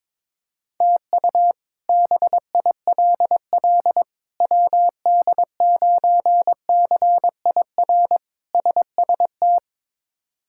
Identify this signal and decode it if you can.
Morse code, heard as a keyed tone, about 22 wpm, 700 Hz. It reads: TU BILL WD9CIR SST
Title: K1USN SST QSO Practice - 22wpm